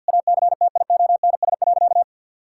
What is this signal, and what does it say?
Morse code, heard as a keyed tone, about 52 wpm, 700 Hz. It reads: ACTIONS.
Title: Sentences from Top 1000 Words - Review Part 6 - 50wpm